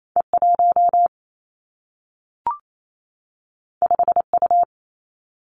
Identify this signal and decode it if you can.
Morse code, heard as a keyed tone, about 28 wpm, 700 Hz. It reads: E 1 5 U
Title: Two Challenging Letter-Number Combinations 28wpm